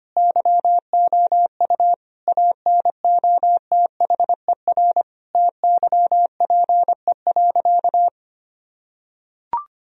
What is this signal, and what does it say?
Morse code, heard as a keyed tone, about 25 wpm, 700 Hz. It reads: YOU ANOTHER TYPE.
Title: Sentences from Top 1000 Words - Review Part 3 - 25wpm